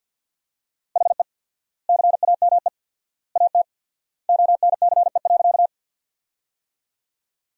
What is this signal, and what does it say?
Morse code, heard as a keyed tone, about 50 wpm, 700 Hz. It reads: HE CAME AT ONCE.
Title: 4x Word-Spacing - Sentences from Top 300 Words - 50wpm